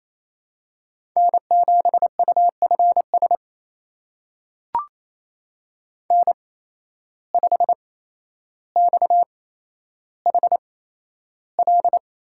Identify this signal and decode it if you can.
Morse code, heard as a keyed tone, about 27 wpm, spaced as for 12 wpm, 700 Hz. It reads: N7UFS N5XHL
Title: USA Call Signs - 5 Characters - 8x Character Spacing - 28wpm